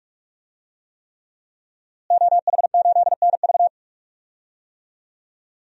Morse code, heard as a keyed tone, about 47 wpm, 700 Hz. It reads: OH8NV
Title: DXCC-100 Call Signs - 5 Characters - Repeated 3x - 45wpm